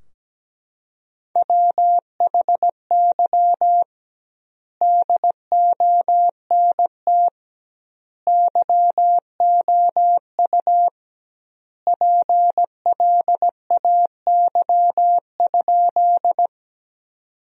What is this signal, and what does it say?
Morse code, heard as a keyed tone, about 17 wpm, 700 Hz. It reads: WHY DONT YOU PLAY?
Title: 2x Word-Spacing - Sentences from Top 300 Words - 17wpm